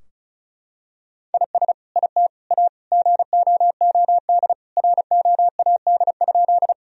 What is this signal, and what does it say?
Morse code, heard as a keyed tone, about 36 wpm, 700 Hz. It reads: IS IT A GOOD ROAD?
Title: Sentences from Top 1000 Words - Review Part 2 - 35wpm